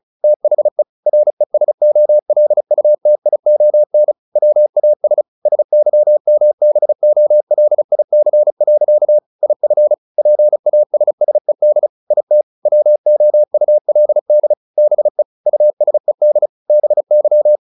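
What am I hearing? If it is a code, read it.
Morse code, heard as a keyed tone, about 34 wpm, 600 Hz. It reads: THE RESOLUTION WAS SYMBOLIC. IF PASSED IT WOULD BE USED BY
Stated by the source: Morse practice at 35 wpm from CW med Gnister, sending Ken Follett's novel Never in English.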